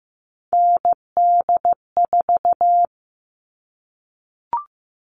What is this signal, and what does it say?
Morse code, heard as a keyed tone, about 15 wpm, 700 Hz. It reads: ND4
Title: All Cut Numbers - 3-Digit 15wpm